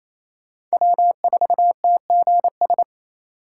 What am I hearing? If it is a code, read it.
Morse code, heard as a keyed tone, about 35 wpm, 700 Hz. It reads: W4TGS